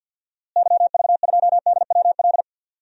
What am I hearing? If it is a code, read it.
Morse code, heard as a keyed tone, about 51 wpm, 700 Hz. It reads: YV2DWL
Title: DXCC-100 Call Signs - 6 Characters - Repeated 3x - 50wpm